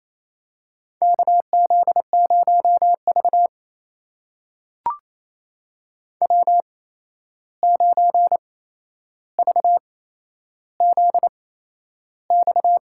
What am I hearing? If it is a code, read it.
Morse code, heard as a keyed tone, about 28 wpm, spaced as for 11 wpm, 700 Hz. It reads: KZ0V W9VZX
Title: USA Call Signs - 8x Character Spacing - 28wpm